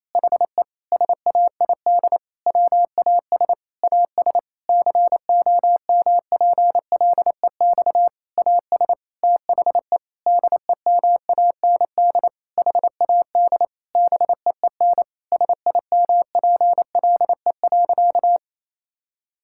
Morse code, heard as a keyed tone, about 28 wpm, 700 Hz. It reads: HE SAID WAS AS COMPLEX AS THE DEMAND HAD BEEN SIMPLE.
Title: Sentences from Top 1000 Words - Review Part 6 - 28wpm